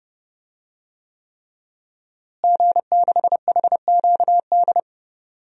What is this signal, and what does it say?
Morse code, heard as a keyed tone, about 30 wpm, 700 Hz. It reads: G6HQD
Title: DXCC-100 Call Signs - 5 Characters - Repeated 3x - 30wpm